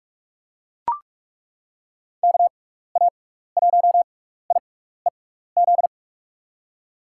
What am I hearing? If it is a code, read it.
Morse code, heard as a keyed tone, about 42 wpm, spaced as for 15 wpm, 700 Hz. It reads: KA1IEZ
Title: USA Call Signs - 6 Characters - 6x Character Spacing - 45wpm